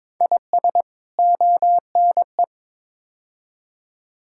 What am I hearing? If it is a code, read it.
Morse code, heard as a keyed tone, about 22 wpm, 700 Hz. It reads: IS ONE
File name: practice / 22wpm / Build A Phrase - 5-Word Phrases - 22wpm